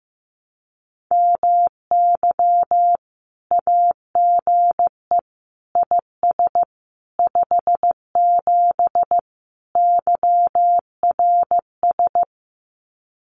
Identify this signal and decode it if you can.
Morse code, heard as a keyed tone, about 15 wpm, 700 Hz. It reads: MY AGE IS 57 YRS
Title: Common QSO Phrases Encore 15wpm